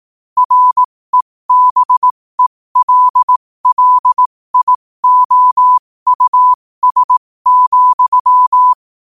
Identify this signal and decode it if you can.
Morse code, heard as a keyed tone, about 18 wpm, spaced as for 15 wpm, 1000 Hz. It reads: REBELLIOUS,
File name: qotdfiles / 15wpm